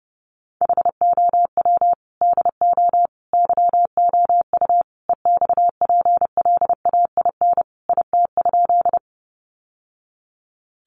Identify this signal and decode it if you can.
Morse code, heard as a keyed tone, about 30 wpm, 700 Hz. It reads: HOW DO YOU EXPLAIN IT?
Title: Sentences from Top 1100 Words 30wpm